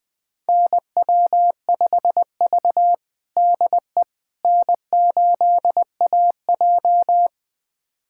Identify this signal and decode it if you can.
Morse code, heard as a keyed tone, about 21 wpm, 700 Hz. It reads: NW5V DE N8AJ
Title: USA Callsign DE Callsign - 4 Characters - 20wpm